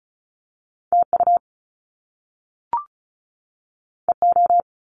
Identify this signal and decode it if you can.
Morse code, heard as a keyed tone, about 36 wpm, 700 Hz. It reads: TU EO
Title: Top Two Letter Combinations in Words 35wpm